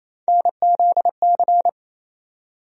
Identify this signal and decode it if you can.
Morse code, heard as a keyed tone, about 28 wpm, 700 Hz. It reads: NZC
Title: USA Call Sign Postfixes - 3 Letters 28wpm